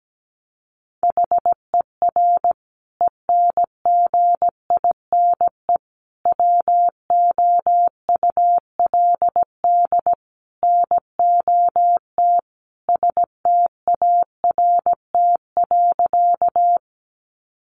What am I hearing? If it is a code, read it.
Morse code, heard as a keyed tone, about 17 wpm, 700 Hz. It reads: HER ENGINE WOULD NOT START.
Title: Sentences from Top 1500 Words 17wpm